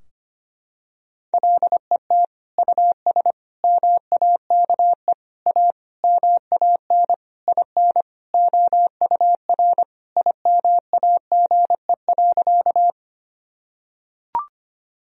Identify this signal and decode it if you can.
Morse code, heard as a keyed tone, about 25 wpm, 700 Hz. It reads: LET US MAKE A MAN IN OUR IMAGE.
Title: Sentences from Top 1000 Words - Review Part 4 - 25wpm